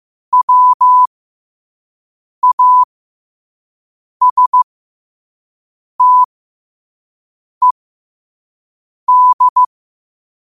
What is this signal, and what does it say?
Morse code, heard as a keyed tone, about 15 wpm, spaced as for 5 wpm, 1000 Hz. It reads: WASTED
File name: qotdfiles / 5wpm